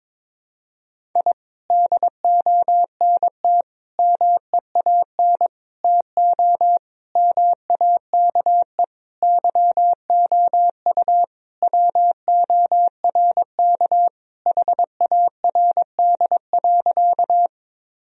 Morse code, heard as a keyed tone, about 22 wpm, 700 Hz. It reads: I DONT MEAN TO MAKE YOU WORK HARD.
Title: Sentences from Top 1000 Words - Review Part 1 - 22wpm